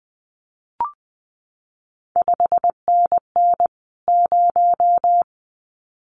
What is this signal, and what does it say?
Morse code, heard as a keyed tone, about 20 wpm, 700 Hz. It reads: 5NN 0
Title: CQ WW DX Contest Exchange Encore - 20wpm